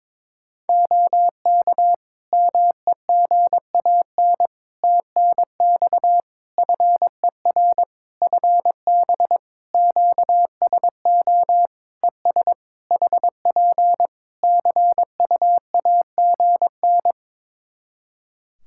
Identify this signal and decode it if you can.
Morse code, heard as a keyed tone, about 22 wpm, 700 Hz. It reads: OK MEGAN TNX FER FB QSO ES HP CUAGN
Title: Practice QSO - Exchange 4 - Ending - 22wpm